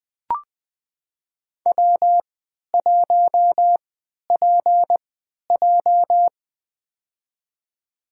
Morse code, heard as a keyed tone, about 20 wpm, spaced as for 11 wpm, 700 Hz. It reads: W1PJ
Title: USA Call Signs - 3x Character Spacing - 20wpm